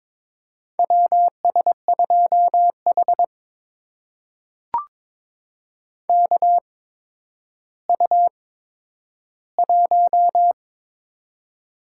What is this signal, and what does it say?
Morse code, heard as a keyed tone, about 22 wpm, spaced as for 11 wpm, 700 Hz. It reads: WS2H KU1
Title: USA Call Signs - 4 Characters - 8x Character Spacing - 22wpm